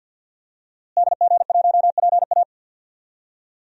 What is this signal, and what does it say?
Morse code, heard as a keyed tone, about 52 wpm, 700 Hz. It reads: DG1PA
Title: DXCC-100 Call Signs - 5 Characters - Repeated 2x - 50wpm